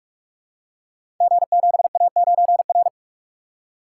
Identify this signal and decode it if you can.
Morse code, heard as a keyed tone, about 43 wpm, 700 Hz. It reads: G7A9R
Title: Code Groups - 5 Letter:Numbers - 45wpm